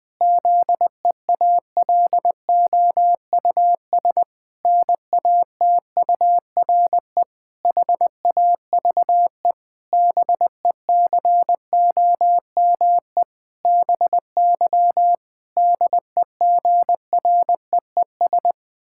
Morse code, heard as a keyed tone, about 20 wpm, 700 Hz. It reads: ZEALOUS NATURE HAVE BECOME BY DEGREES
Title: Pangram Challenge v2 - 20wpm